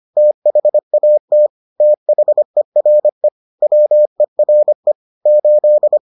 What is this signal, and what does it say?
Morse code, heard as a keyed tone, about 25 wpm, 600 Hz. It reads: THAT THERE WERE 8